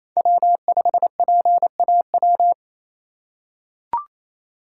Morse code, heard as a keyed tone, about 28 wpm, 700 Hz. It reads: W5PAW
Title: USA Call Signs - 5 Characters - 2x Character Spacing - 28wpm